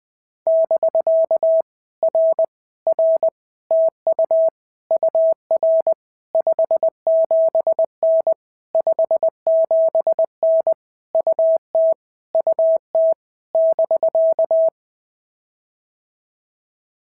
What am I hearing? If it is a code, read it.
Morse code, heard as a keyed tone, about 21 wpm, 650 Hz. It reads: <BK> R R TU UR 57N 57N UT UT <BK>
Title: POTA QSO Practice - 20wpm